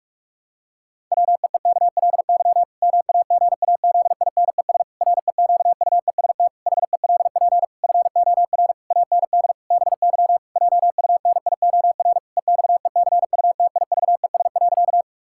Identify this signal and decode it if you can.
Morse code, heard as a keyed tone, about 41 wpm, 700 Hz. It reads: WEEKLY MAGAZINES REQUEST HELP FOR AND BY JUNIOR EXECUTIVES.
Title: Pangram Challenge v2 - 45wpm